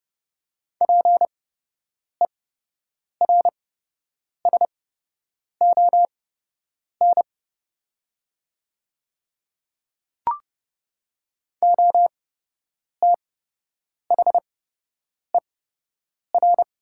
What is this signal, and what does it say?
Morse code, heard as a keyed tone, about 30 wpm, spaced as for 8 wpm, 700 Hz. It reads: PERSON OTHER
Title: 8x Character Spacing - Top 500 Words - 30wpm